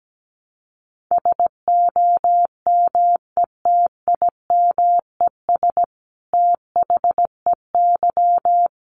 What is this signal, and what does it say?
Morse code, heard as a keyed tone, about 17 wpm, 700 Hz. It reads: SOMETIMES THEY
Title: Sentences from Top 600 Words 17wpm